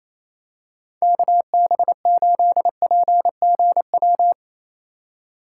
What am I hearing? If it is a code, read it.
Morse code, heard as a keyed tone, about 28 wpm, 700 Hz. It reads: KB8PGW
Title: K1USN SST QSO Practice - 28wpm